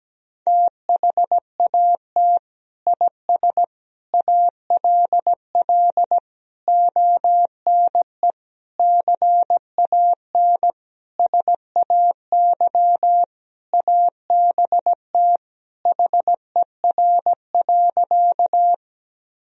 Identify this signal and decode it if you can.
Morse code, heard as a keyed tone, about 17 wpm, 700 Hz. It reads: THAT IS ALL ONE CAN SAY ABT HER.
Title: Sentences from Top 200 Words and Abbreviations 17wpm